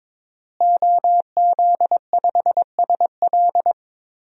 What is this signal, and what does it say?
Morse code, heard as a keyed tone, about 22 wpm, 700 Hz. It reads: OZ5SL
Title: DXCC-100 Call Signs - 5 Characters - Repeated 2x - 22wpm